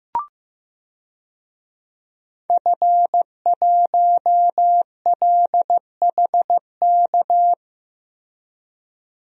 Morse code, heard as a keyed tone, about 15 wpm, 700 Hz. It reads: F1LHK
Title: DXCC-100 Call Signs - 5 Characters - Repeated 2x - 15wpm